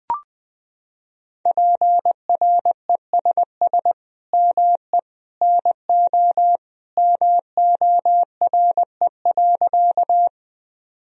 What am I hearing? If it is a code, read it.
Morse code, heard as a keyed tone, about 21 wpm, 700 Hz. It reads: PRESS ME NO MORE.
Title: Sentences from Top 1000 Words - Review Part 1 - 20wpm